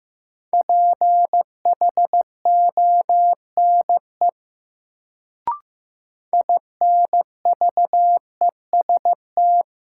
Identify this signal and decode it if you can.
Morse code, heard as a keyed tone, about 15 wpm, 700 Hz. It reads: PHONE INVEST
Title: Sentences from Top 1100 Words - New Word Drill - 15wpm